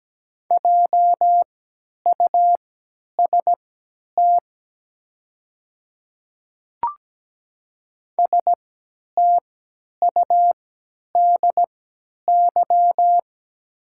Morse code, heard as a keyed tone, about 17 wpm, spaced as for 10 wpm, 700 Hz. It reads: JUST STUDY
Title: 3x Character Spacing - Top 300 Words - 17wpm